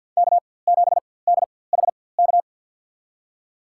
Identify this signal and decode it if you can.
Morse code, heard as a keyed tone, about 47 wpm, spaced as for 23 wpm, 700 Hz. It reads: K7DHK
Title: USA Call Signs - 5 Characters - 4x Character Spacing - 50wpm